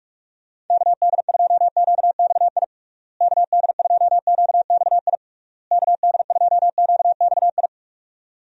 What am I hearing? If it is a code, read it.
Morse code, heard as a keyed tone, about 46 wpm, 700 Hz. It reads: KD2QXI KD2QXI KD2QXI
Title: USA Call Signs - 6 Characters - Repeated 3x - 45wpm